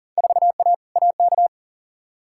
Morse code, heard as a keyed tone, about 40 wpm, 700 Hz. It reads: 4A AK
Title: Field Day Exchanges - 40wpm